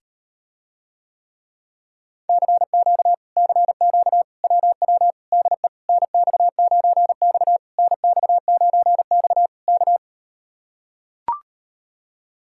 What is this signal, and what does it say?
Morse code, heard as a keyed tone, about 39 wpm, 700 Hz. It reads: CQ CQ WW DE NX9X NX9X K